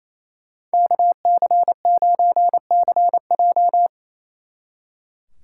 Morse code, heard as a keyed tone, about 35 wpm, 700 Hz. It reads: KC9CJ